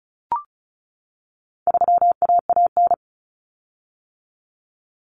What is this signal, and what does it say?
Morse code, heard as a keyed tone, about 34 wpm, 700 Hz. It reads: 3AAN
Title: Common Cut Numbers - 4-Digit 35wpm